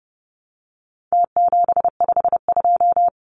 Morse code, heard as a keyed tone, about 30 wpm, 700 Hz. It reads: T752
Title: Numbers - Contest Style - 4-Digit 30wpm